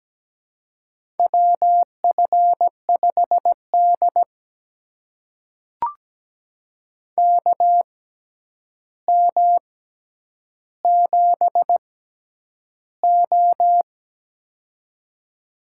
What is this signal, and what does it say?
Morse code, heard as a keyed tone, about 17 wpm, spaced as for 9 wpm, 700 Hz. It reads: WF5D KM7O
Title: USA Call Signs - 4 Characters - 6x Character Spacing - 17wpm